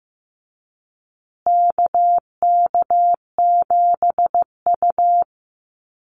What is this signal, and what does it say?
Morse code, heard as a keyed tone, about 15 wpm, 700 Hz. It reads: KK7U
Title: USA Call Signs Repeated 2x 15wpm